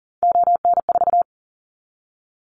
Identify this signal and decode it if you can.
Morse code, heard as a keyed tone, about 40 wpm, 700 Hz. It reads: ON4
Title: Callsign Prefixes for Top 25 DXCC - Repeated 3x - 40wpm